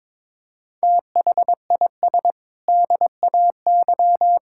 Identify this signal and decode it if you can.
Morse code, heard as a keyed tone, about 22 wpm, 700 Hz. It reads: THIS DAY
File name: practice / 22wpm / Sentences with 8-Word Phrases - 22wpm